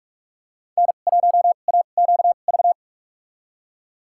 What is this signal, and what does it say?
Morse code, heard as a keyed tone, about 44 wpm, spaced as for 32 wpm, 700 Hz. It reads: N1AQV
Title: USA Call Signs - 5 Characters - 2x Character Spacing - 45wpm